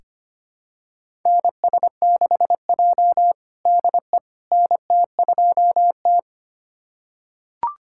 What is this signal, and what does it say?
Morse code, heard as a keyed tone, about 25 wpm, 700 Hz. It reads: NS6J DE NT2T
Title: USA Callsign DE Callsign - 4 Characters - 25wpm